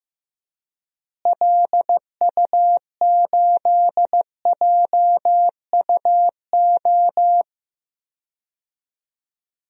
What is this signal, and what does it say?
Morse code, heard as a keyed tone, about 15 wpm, 700 Hz. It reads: LU8JUO
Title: DXCC-100 Call Signs - 6 Characters - Repeated 3x - 15wpm